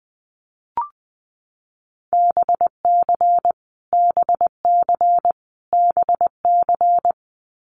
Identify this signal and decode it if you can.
Morse code, heard as a keyed tone, about 20 wpm, 700 Hz. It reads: BC BC BC
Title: Canadian Provinces and Territories - Repeated 3x - 20wpm